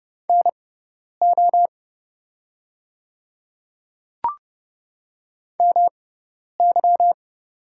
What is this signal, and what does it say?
Morse code, heard as a keyed tone, about 30 wpm, spaced as for 10 wpm, 700 Hz. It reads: NO MY
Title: 6x Character Spacing - Top 100 Words - 30wpm